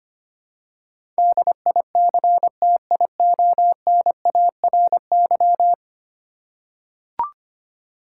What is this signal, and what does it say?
Morse code, heard as a keyed tone, about 25 wpm, 700 Hz. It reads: DICTIONARY
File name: 10 Letter Words - First 3 Letters Given - 25wpm